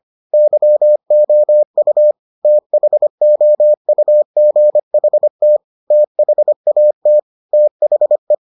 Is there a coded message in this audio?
Yes: Morse code from 0.3 to 8.3 s, heard as a keyed tone, about 25 wpm, 600 Hz. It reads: YOU THOUGHT THAT THE